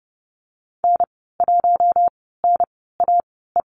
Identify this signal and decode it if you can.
Morse code, heard as a keyed tone, about 30 wpm, spaced as for 17 wpm, 700 Hz. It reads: N1NAE
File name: Lesson 19g - Letter w - Call Signs 30wpm